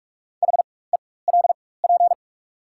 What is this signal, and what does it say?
Morse code, heard as a keyed tone, about 48 wpm, spaced as for 21 wpm, 690 Hz. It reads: HELP
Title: 4x Character Spacing - Top 200 Words - 45wpm